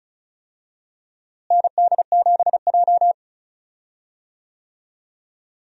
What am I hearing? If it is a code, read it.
Morse code, heard as a keyed tone, about 36 wpm, 700 Hz. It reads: ND7J
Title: USA Call Signs - 4 Characters - Repeated 3x - 35wpm